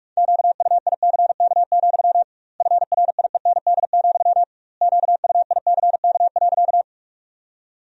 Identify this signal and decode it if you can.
Morse code, heard as a keyed tone, about 46 wpm, 700 Hz. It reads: QUICK, FRIEND, QUICK.
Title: 2x Word-Spacing - Sentences from Top 500 Words - 45wpm